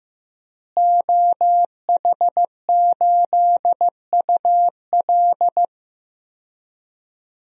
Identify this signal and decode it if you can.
Morse code, heard as a keyed tone, about 15 wpm, 700 Hz. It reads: OH8UL
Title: DXCC-100 Call Signs - 5 Characters - Repeated 2x - 15wpm